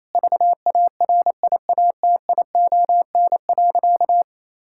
Morse code, heard as a keyed tone, about 27 wpm, 700 Hz. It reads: VARIATION.
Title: Sentences from Top 1600 Words 28wpm